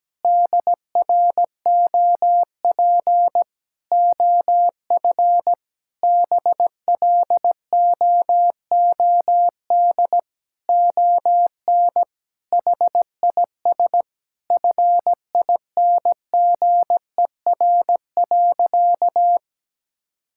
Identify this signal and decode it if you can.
Morse code, heard as a keyed tone, about 17 wpm, 700 Hz. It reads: DROP OF BLOOD ON HIS FINGER.
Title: Sentences from Top 1600 Words 17wpm